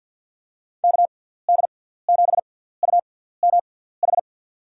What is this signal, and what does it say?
Morse code, heard as a keyed tone, about 50 wpm, spaced as for 17 wpm, 700 Hz. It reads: KD7UMH
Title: USA Call Signs - 6 Characters - 6x Character Spacing - 50wpm